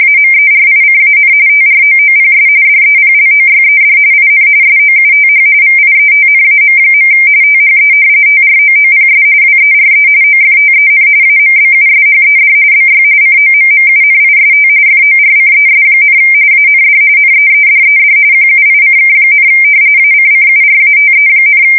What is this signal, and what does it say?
RTTY, about 45.45 baud. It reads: AMERICAN TROOPS DEPLOYED IN SOUTH VIETNAM. THE TROOPS WERE OFTEN AMBUSHED BY COMMUNIST FORCES, AND NEEDED QUICK, ACCURATE,
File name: fsk170 / battleshipnj